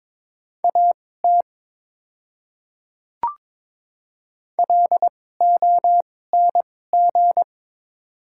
Morse code, heard as a keyed tone, about 22 wpm, spaced as for 16 wpm, 700 Hz. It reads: AT LONG
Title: 2x Character Spacing - Top 200 Words - 22wpm